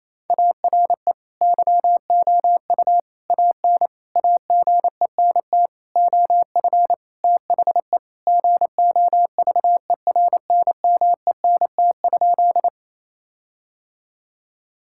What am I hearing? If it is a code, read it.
Morse code, heard as a keyed tone, about 27 wpm, 700 Hz. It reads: ARE YOU AN AGENT OF THE GOVERNMENT?
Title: Sentences from Top 1600 Words 28wpm